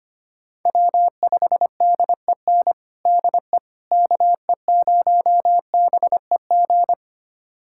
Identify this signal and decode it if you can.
Morse code, heard as a keyed tone, about 25 wpm, 700 Hz. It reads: W5DEN DE KE0BEG
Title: USA Callsign DE Callsign 25wpm